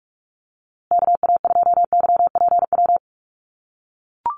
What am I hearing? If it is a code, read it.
Morse code, heard as a keyed tone, about 44 wpm, 700 Hz. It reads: KA2YPW
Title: USA Call Signs Repeated 3x 45wpm